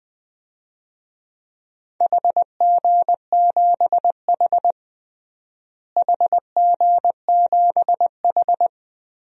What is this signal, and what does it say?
Morse code, heard as a keyed tone, about 20 wpm, 700 Hz. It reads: HG7H HG7H
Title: DXCC-100 Call Signs - 4 Characters - Repeated 3x - 20wpm